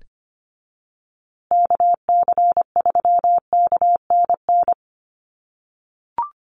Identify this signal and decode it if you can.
Morse code, heard as a keyed tone, about 25 wpm, 700 Hz. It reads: KC3KNN
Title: USA Call Signs Repeated 3x 25wpm